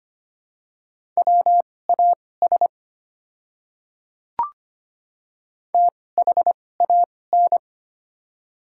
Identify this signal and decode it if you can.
Morse code, heard as a keyed tone, about 25 wpm, spaced as for 18 wpm, 700 Hz. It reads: WAS THAN
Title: 2x Character Spacing - Top 200 Words - 25wpm